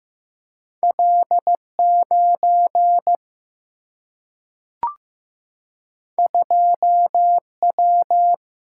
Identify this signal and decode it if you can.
Morse code, heard as a keyed tone, about 15 wpm, 700 Hz. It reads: L9 2W
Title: Single-Letter Single-Number 15wpm